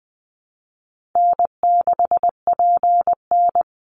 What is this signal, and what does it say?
Morse code, heard as a keyed tone, about 20 wpm, 700 Hz. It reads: N6PN
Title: Easy Call Signs Repeated 3x - 20wpm